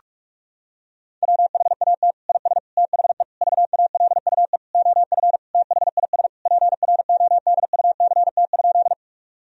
Morse code, heard as a keyed tone, about 46 wpm, 700 Hz. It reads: WHAT IS THE VALUE OF THIS PRODUCT?
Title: Sentences from Top 1000 Words - Review Part 3 - 45wpm